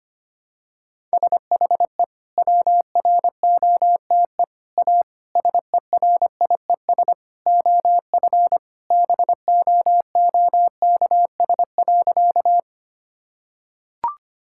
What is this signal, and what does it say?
Morse code, heard as a keyed tone, about 25 wpm, 700 Hz. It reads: SHE WROTE A SERIES OF BOOKS.
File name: practice / 25wpm / Sentences with 3-Word Phrases - 25wpm